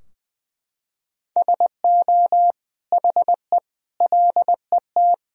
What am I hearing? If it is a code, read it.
Morse code, heard as a keyed tone, about 20 wpm, 700 Hz. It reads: SO HE LET
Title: Sentences from Top 1000 Words - Review Part 2 - 20wpm